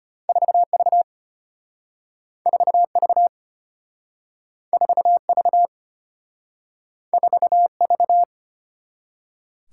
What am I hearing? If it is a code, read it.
Morse code, heard as a keyed tone, about 34 wpm, 700 Hz. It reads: 4V 4V 4V 4V